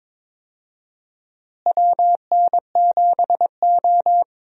Morse code, heard as a keyed tone, about 22 wpm, 700 Hz. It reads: WN7O
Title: USA Call Signs - 4 Characters - Repeated 3x - 22wpm